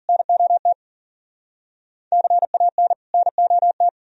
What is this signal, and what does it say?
Morse code, heard as a keyed tone, about 52 wpm, 700 Hz. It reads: NOT CAN NOT